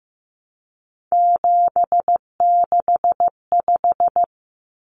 Morse code, heard as a keyed tone, about 15 wpm, 700 Hz. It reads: 765
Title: Numbers - 3-Digit 15wpm